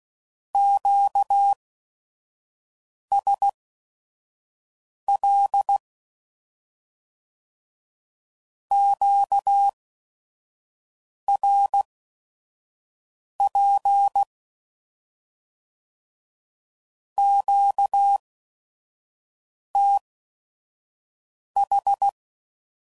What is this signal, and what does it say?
Morse code, heard as a keyed tone, about 16 wpm, spaced as for 5 wpm, 800 Hz. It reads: QSL QRP QTH
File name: CW Code Groups @5WPM/Q Calls